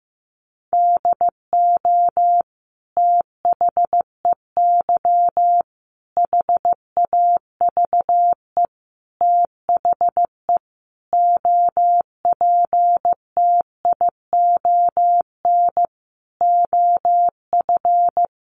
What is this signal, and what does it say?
Morse code, heard as a keyed tone, about 14 wpm, 700 Hz. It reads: DO THEY HAVE THE OPTION OF
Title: Sentences from Top 1300 Words 15wpm